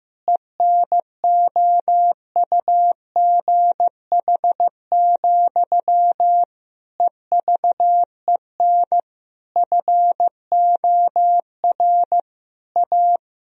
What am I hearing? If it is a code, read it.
Morse code, heard as a keyed tone, about 15 wpm, 700 Hz. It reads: ENOUGH, EVEN FOR A
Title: Sentences from Top 1000 Words - Review Part 2 - 15wpm